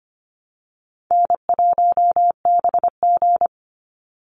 Morse code, heard as a keyed tone, about 25 wpm, 700 Hz. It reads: N1BG
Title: Easy Call Signs - 25wpm